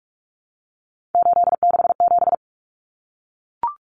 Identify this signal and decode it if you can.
Morse code, heard as a keyed tone, about 44 wpm, 700 Hz. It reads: 867
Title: Common Cut Numbers - 3-Digit 45wpm